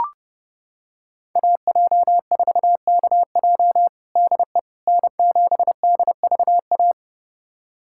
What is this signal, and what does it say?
Morse code, heard as a keyed tone, about 30 wpm, 700 Hz. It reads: AJ4KJ DE N7DVA
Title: USA Callsign DE Callsign - 5 Characters - 30wpm